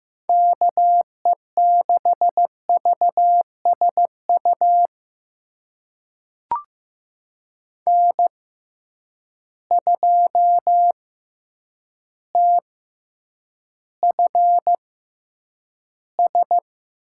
Morse code, heard as a keyed tone, about 15 wpm, spaced as for 8 wpm, 700 Hz. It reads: KE6VSU N2TFS
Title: USA Call Signs - 6x Character Spacing - 15wpm